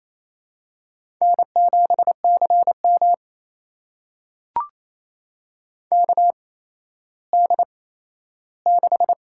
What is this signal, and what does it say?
Morse code, heard as a keyed tone, about 28 wpm, spaced as for 14 wpm, 700 Hz. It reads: N7CM KD6
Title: USA Call Signs - 4 Characters - 8x Character Spacing - 28wpm